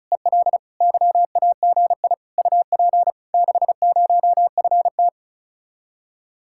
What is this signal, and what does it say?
Morse code, heard as a keyed tone, about 44 wpm, 700 Hz. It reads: EL YAGI UP 60FT